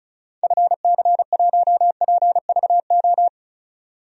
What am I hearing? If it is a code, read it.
Morse code, heard as a keyed tone, about 36 wpm, 700 Hz. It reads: FC1PVO
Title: DXCC-100 Call Signs - 6 Characters - 35wpm